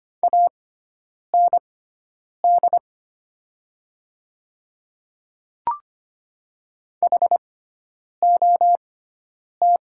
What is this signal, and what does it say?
Morse code, heard as a keyed tone, about 25 wpm, spaced as for 9 wpm, 700 Hz. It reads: AND HOT
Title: 6x Character Spacing - Top 100 Words - 25wpm